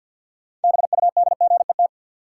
Morse code, heard as a keyed tone, about 53 wpm, 700 Hz. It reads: BUDGET
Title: Sentences from Top 1200 Words - New Word Drill Alt - 50wpm